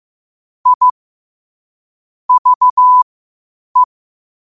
Morse code, heard as a keyed tone, about 15 wpm, spaced as for 8 wpm, 1000 Hz. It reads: I VE
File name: qotdfiles / 8wpm